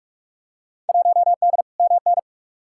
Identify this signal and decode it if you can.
Morse code, heard as a keyed tone, about 43 wpm, 700 Hz. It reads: 1D MN
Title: Field Day Exchanges - 45wpm